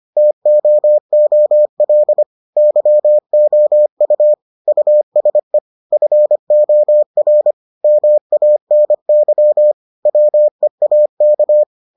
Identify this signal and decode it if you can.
Morse code, heard as a keyed tone, about 25 wpm, 600 Hz. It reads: TOOL YOU USE FOR MANY WEAK